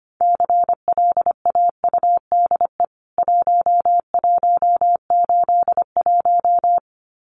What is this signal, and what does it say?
Morse code, heard as a keyed tone, about 25 wpm, 700 Hz. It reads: CLAUDE 1181
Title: CWT Exchange 25wpm